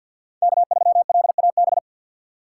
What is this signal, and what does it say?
Morse code, heard as a keyed tone, about 52 wpm, 690 Hz. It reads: K3LAB
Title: USA Call Signs - 6x Character Spacing - 50wpm